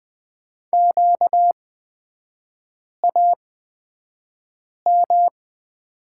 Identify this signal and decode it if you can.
Morse code, heard as a keyed tone, about 20 wpm, spaced as for 5 wpm, 700 Hz. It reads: QAM
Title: 236 Lesson 38e - Letter q - Three Characters 20wpm